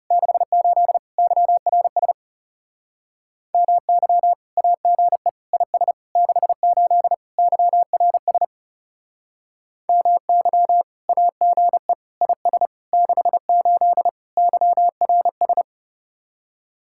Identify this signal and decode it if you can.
Morse code, heard as a keyed tone, about 36 wpm, 700 Hz. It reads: 68 YRS MY AGE IS 68 YRS MY AGE IS 68 YRS
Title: Speed-Racing Common QSO Phrases - 30wpm